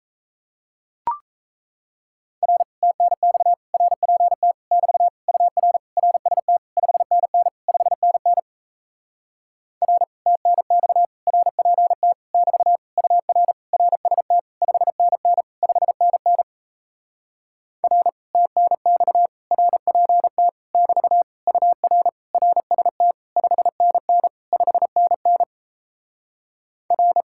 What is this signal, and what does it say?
Morse code, heard as a keyed tone, about 38 wpm, 700 Hz. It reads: R TNX RPT <BT> UR RST 5NN 5NN R TNX RPT <BT> UR RST 5NN 5NN R TNX RPT <BT> UR RST 5NN 5NN R